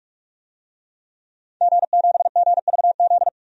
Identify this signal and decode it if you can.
Morse code, heard as a keyed tone, about 41 wpm, 700 Hz. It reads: G7GVZ
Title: DXCC-100 Call Signs - 5 Characters - 45wpm